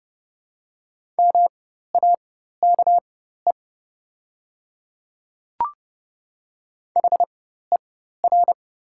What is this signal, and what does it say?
Morse code, heard as a keyed tone, about 30 wpm, spaced as for 14 wpm, 700 Hz. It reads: MAKE HER
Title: 4x Character Spacing - Top 100 Words - 30wpm